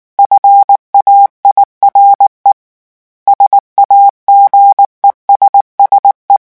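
Morse code, heard as a keyed tone, about 19 wpm, 800 Hz. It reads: FAIRE SAGESSE
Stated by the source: French Morse revision 8 drill of random five-letter groups at 18 words per minute.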